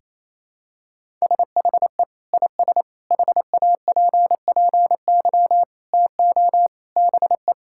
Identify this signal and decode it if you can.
Morse code, heard as a keyed tone, about 28 wpm, 700 Hz. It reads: SHE IS HAPPY TO BE
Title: Sentences from Top 1000 Words - Review Part 6 - 28wpm